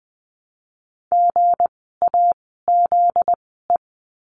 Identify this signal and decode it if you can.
Morse code, heard as a keyed tone, about 20 wpm, spaced as for 15 wpm, 700 Hz. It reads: GAZE
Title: 244 Lesson 39f - Letter z - Words 20wpm